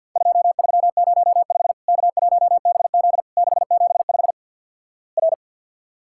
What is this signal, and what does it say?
Morse code, heard as a keyed tone, about 49 wpm, 700 Hz. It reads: 2305 K1BZ 675 R
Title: Worked All Europe DX Contest - QTC Exchange - 50wpm